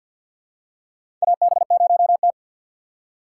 Morse code, heard as a keyed tone, about 53 wpm, 700 Hz. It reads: AB0T
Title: USA Call Signs - 4 Characters - 2x Character Spacing - 50wpm